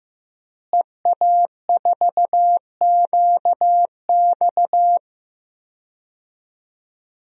Morse code, heard as a keyed tone, about 15 wpm, 700 Hz. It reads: EA4QX
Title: DXCC-100 Call Signs - 5 Characters - Repeated 3x - 15wpm